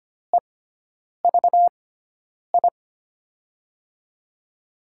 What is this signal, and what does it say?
Morse code, heard as a keyed tone, about 25 wpm, spaced as for 9 wpm, 700 Hz. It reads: EVI
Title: USA Call Signs - 5 Characters - 6x Character Spacing - 25wpm